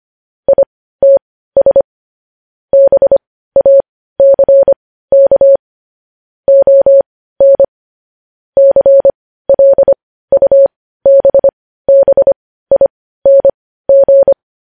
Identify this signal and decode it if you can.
Morse code, heard as a keyed tone, about 25 wpm, spaced as for 15 wpm, 550 Hz. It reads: ITS BACK ON CLUBBING
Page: News Headlines in Morse Code at 15 WPM